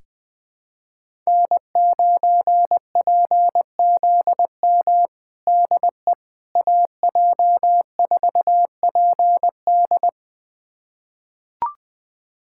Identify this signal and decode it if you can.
Morse code, heard as a keyed tone, about 20 wpm, 700 Hz. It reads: N9PZM DE AJ4PD
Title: USA Callsign DE Callsign - 5 Characters - 20wpm